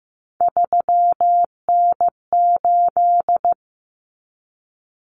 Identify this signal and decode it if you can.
Morse code, heard as a keyed tone, about 15 wpm, 700 Hz. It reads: 3N8
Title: Common Cut Numbers - 3-Digit 15wpm